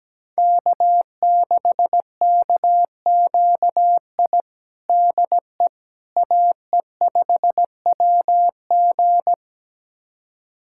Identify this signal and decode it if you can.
Morse code, heard as a keyed tone, about 17 wpm, 700 Hz. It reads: K6KQI DE AE5WG
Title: USA Callsign DE Callsign - 5 Characters - 17wpm